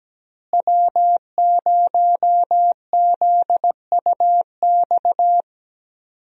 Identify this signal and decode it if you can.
Morse code, heard as a keyed tone, about 17 wpm, 700 Hz. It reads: W0ZUX
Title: USA Call Signs - 5 Characters - Repeated 2x - 17wpm